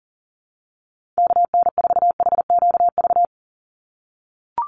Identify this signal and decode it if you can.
Morse code, heard as a keyed tone, about 40 wpm, 700 Hz. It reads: KN4HQV
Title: USA Call Signs 40wpm